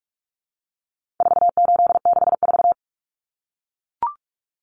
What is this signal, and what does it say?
Morse code, heard as a keyed tone, about 44 wpm, 700 Hz. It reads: 4864
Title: Numbers - 4-Digit 45wpm